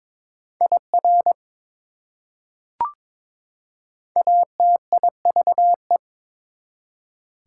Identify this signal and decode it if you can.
Morse code, heard as a keyed tone, about 22 wpm, 700 Hz. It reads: IR ATIVE
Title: Prefixes and Suffixes 22wpm